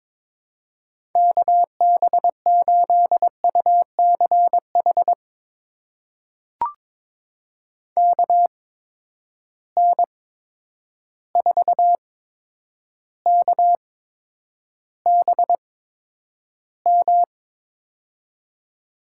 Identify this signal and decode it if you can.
Morse code, heard as a keyed tone, about 22 wpm, spaced as for 9 wpm, 700 Hz. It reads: KB8UCH KN4KBM
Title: USA Call Signs - 6 Characters - 8x Character Spacing - 22wpm